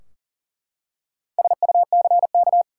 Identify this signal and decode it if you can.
Morse code, heard as a keyed tone, about 40 wpm, 700 Hz. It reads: SUCK
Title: Four Letter Words - 40wpm